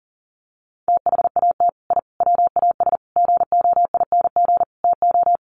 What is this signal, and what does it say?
Morse code, heard as a keyed tone, about 40 wpm, 700 Hz. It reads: THAT I WAS GOING TO
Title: Sets of 5 Words Encore 40wpm